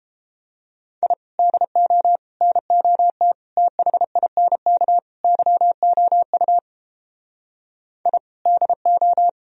Morse code, heard as a keyed tone, about 33 wpm, 700 Hz. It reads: I DO NOT THINK YOU I DO